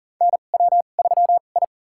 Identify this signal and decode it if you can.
Morse code, heard as a keyed tone, about 37 wpm, spaced as for 28 wpm, 700 Hz. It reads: NW3I
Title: USA Call Signs - 4 Characters - 2x Character Spacing - 40wpm